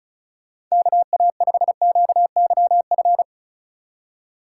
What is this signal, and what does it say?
Morse code, heard as a keyed tone, about 34 wpm, 700 Hz. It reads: KA5QYF
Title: USA Call Signs - 6 Characters - 2x Character Spacing - 35wpm